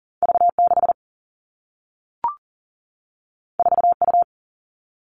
Morse code, heard as a keyed tone, about 38 wpm, 700 Hz. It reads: V 6 4 U
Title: Two Challenging Letter-Number Combinations 40wpm